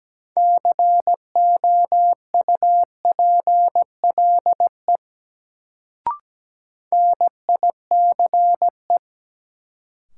Morse code, heard as a keyed tone, about 17 wpm, 700 Hz. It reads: COUPLE NICE
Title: Sentences from Top 1100 Words - New Word Drill - 17wpm